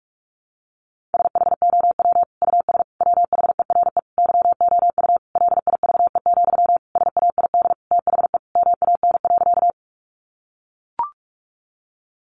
Morse code, heard as a keyed tone, about 44 wpm, 700 Hz. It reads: SHOW US WHERE YOU LIVE, SAID THE MAN.
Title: Sentences from Top 200 Words 45wpm